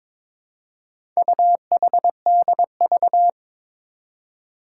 Morse code, heard as a keyed tone, about 22 wpm, 700 Hz. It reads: UHDV
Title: Four Challenging Letter-Number Combinations 22wpm